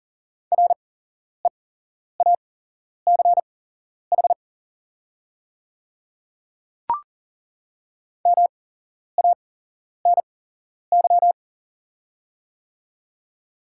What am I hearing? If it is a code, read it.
Morse code, heard as a keyed tone, about 40 wpm, spaced as for 11 wpm, 690 Hz. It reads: REACH MANY
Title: 8x Character Spacing - Top 500 Words - 40wpm